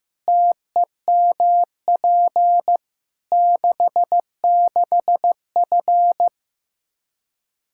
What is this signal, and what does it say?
Morse code, heard as a keyed tone, about 15 wpm, 700 Hz. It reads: TEMP 66F
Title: Practice QSO - Exchange 2 - Info - 15wpm